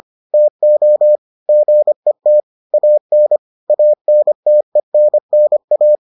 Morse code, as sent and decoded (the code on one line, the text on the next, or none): - --- / --. - / .- -. / .- -. - . -. -. .-
TO GET AN ANTENNA